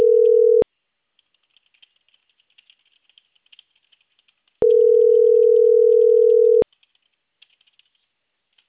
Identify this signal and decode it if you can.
Ringback tone (North America), 440 + 480 Hz.